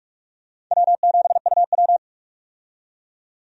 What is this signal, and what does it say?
Morse code, heard as a keyed tone, about 46 wpm, 700 Hz. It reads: W7UW
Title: USA Call Signs - 4 Characters - 45wpm